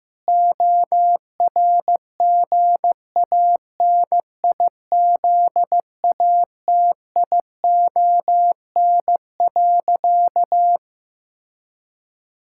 Morse code, heard as a keyed tone, about 15 wpm, 700 Hz. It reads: ORGANIZATION.
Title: Sentences from Top 1000 Words - Review Part 5 - 15wpm